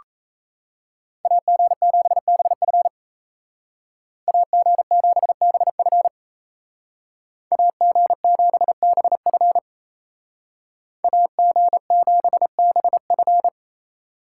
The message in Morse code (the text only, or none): AG7BF AG7BF AG7BF AG7BF